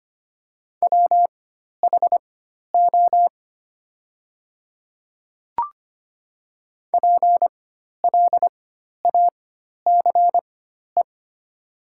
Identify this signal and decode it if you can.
Morse code, heard as a keyed tone, about 25 wpm, spaced as for 12 wpm, 700 Hz. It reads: WHO PLACE
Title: 4x Character Spacing - Top 300 Words - 25wpm